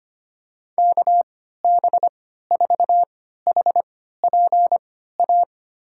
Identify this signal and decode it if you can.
Morse code, heard as a keyed tone, about 25 wpm, spaced as for 14 wpm, 700 Hz. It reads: KB4HPA
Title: USA Call Signs - 3x Character Spacing - 25wpm